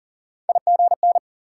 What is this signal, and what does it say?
Morse code, heard as a keyed tone, about 55 wpm, spaced as for 48 wpm, 700 Hz. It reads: IGN
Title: Speed-Racing Multi-letter Phonemes - 40wpm